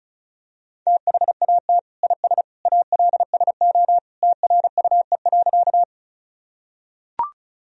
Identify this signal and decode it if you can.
Morse code, heard as a keyed tone, about 36 wpm, 700 Hz. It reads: THAT IS ALSO TRUE.
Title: Sentences from Top 1000 Words - Review Part 1 - 35wpm